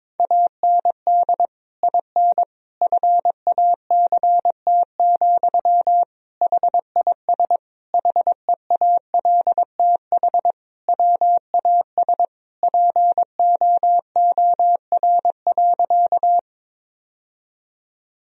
Morse code, heard as a keyed tone, about 22 wpm, 700 Hz. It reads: AND IN FACT, HIS HEALTH WAS POOR.
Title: Sentences from Top 1000 Words - Review Part 3 - 22wpm